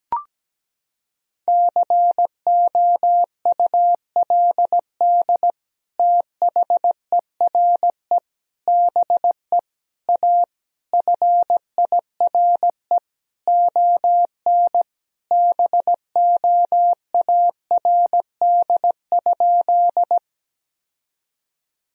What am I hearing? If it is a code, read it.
Morse code, heard as a keyed tone, about 17 wpm, 700 Hz. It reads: COULD THERE BE A FIRE ON BOARD?
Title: Sentences from Top 1000 Words - Review Part 4 - 17wpm